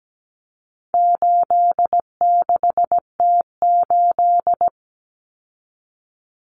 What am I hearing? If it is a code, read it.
Morse code, heard as a keyed tone, about 17 wpm, 700 Hz. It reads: 86T8
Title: Common Cut Numbers - 4-Digit 17wpm